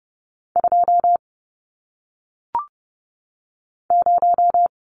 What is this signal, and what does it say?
Morse code, heard as a keyed tone, about 30 wpm, 700 Hz. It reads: 2 0
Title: Numbers - 1-Digit 30wpm